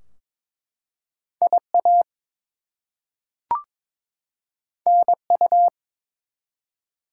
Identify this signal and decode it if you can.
Morse code, heard as a keyed tone, about 22 wpm, 700 Hz. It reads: IA NU
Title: US States and Canadian Provinces 22wpm